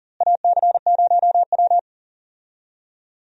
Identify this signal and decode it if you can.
Morse code, heard as a keyed tone, about 40 wpm, 700 Hz. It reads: AC0W
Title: USA Call Signs - 4 Characters - Repeated 3x - 40wpm